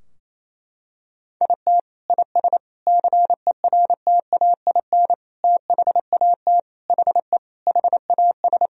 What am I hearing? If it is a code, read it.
Morse code, heard as a keyed tone, about 27 wpm, 700 Hz. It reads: IT IS CERTAIN THAT HE HAS
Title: Sentences from Top 1000 Words - Review Part 2 - 28wpm